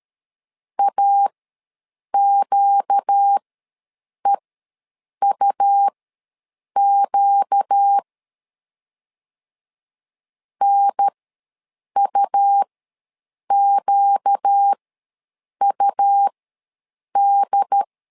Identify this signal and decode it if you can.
Morse code, heard as a keyed tone, about 13 wpm, spaced as for 7 wpm, 800 Hz. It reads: AQEUQ NUQUD
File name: LEZIONI / 4 Q ws 12wpm 1912